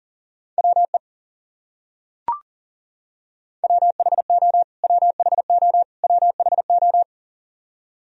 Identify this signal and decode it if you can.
Morse code, heard as a keyed tone, about 40 wpm, 700 Hz. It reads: WE WHO WHO WHO
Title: Top 100 Words Repeated 40wpm